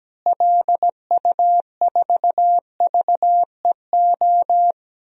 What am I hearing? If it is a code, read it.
Morse code, heard as a keyed tone, about 17 wpm, 700 Hz. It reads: LU4VEO
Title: DXCC-100 Call Signs - 6 Characters - Repeated 2x - 17wpm